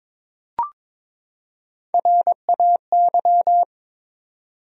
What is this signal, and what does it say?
Morse code, heard as a keyed tone, about 22 wpm, 700 Hz. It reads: RAY
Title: USA Common Names - Max 4 Letters - 22wpm